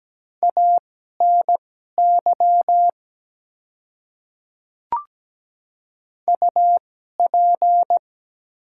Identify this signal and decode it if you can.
Morse code, heard as a keyed tone, about 17 wpm, spaced as for 12 wpm, 700 Hz. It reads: ANY UP
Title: 2x Character Spacing - Top 300 Words - 17wpm